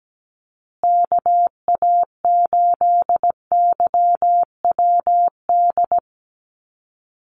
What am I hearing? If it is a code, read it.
Morse code, heard as a keyed tone, about 17 wpm, 700 Hz. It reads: KA8YWD
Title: USA Call Signs Repeated 2x 17wpm